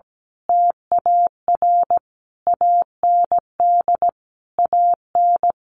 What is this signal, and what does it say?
Morse code, heard as a keyed tone, about 17 wpm, 700 Hz. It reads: TAR AND AN
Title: A Sound Of Thunder 17wpm